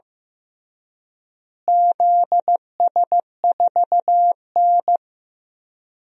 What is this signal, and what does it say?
Morse code, heard as a keyed tone, about 15 wpm, 700 Hz. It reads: ZS4N
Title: DXCC-100 Call Signs - 4 Characters - Repeated 3x - 15wpm